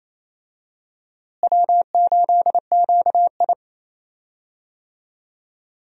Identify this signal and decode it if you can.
Morse code, heard as a keyed tone, about 28 wpm, 700 Hz. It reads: W8QI
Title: USA Call Signs - 4 Characters - Repeated 3x - 28wpm